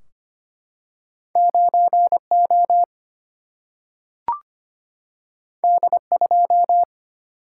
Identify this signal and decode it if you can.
Morse code, heard as a keyed tone, about 25 wpm, 700 Hz. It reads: 9O D2
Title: Single-Letter Single-Number 25wpm